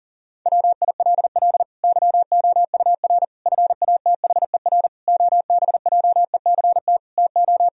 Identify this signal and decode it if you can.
Morse code, heard as a keyed tone, about 37 wpm, 700 Hz. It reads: WILL YOUR FATHER OBJECT TO
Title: Sentences from Top 1000 Words - Review Part 3 - 40wpm